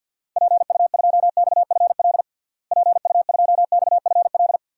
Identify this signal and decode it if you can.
Morse code, heard as a keyed tone, about 48 wpm, 700 Hz. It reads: PU2XFL PU2XFL
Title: DXCC-100 Call Signs - 6 Characters - Repeated 2x - 50wpm